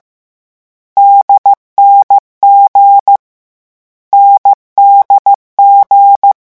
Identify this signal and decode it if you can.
Morse code, heard as a keyed tone, about 15 wpm, 800 Hz. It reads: DNG NDG
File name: morse mp3 / lecon4-vitesse 14